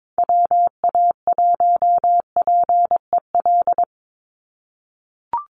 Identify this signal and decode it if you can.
Morse code, heard as a keyed tone, about 22 wpm, 700 Hz. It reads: WA1PEL
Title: USA Call Signs 22wpm